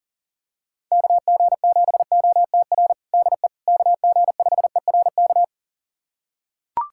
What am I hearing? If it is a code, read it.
Morse code, heard as a keyed tone, about 40 wpm, 700 Hz. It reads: KG7OTR DE KG5ERK
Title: USA Callsign DE Callsign - 6 Characters - 40wpm